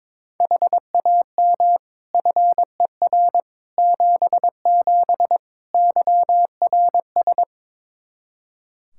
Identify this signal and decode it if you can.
Morse code, heard as a keyed tone, about 21 wpm, 700 Hz. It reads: HAM FER 77 YRS
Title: K1USN SST QSO Practice 22wpm